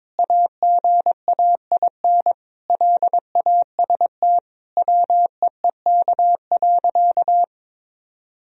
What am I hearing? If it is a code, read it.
Morse code, heard as a keyed tone, about 22 wpm, 700 Hz. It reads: AGAIN LAST WEEK.
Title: Sentences from Top 1000 Words - Review Part 2 - 22wpm